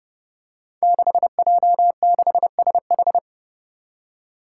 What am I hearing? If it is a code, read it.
Morse code, heard as a keyed tone, about 32 wpm, 700 Hz. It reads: 6J6SH
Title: Five Challenging Letter-Number Combinations 30wpm